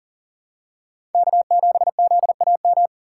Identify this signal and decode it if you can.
Morse code, heard as a keyed tone, about 40 wpm, 700 Hz. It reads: K7ZAM
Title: USA Call Signs - 5 Characters - 3x Character Spacing - 40wpm